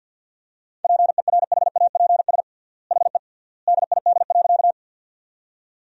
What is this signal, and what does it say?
Morse code, heard as a keyed tone, about 50 wpm, 700 Hz. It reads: PERHAPS HE DID.
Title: 3x Word-Spacing - Sentences from Top 500 Words - 50wpm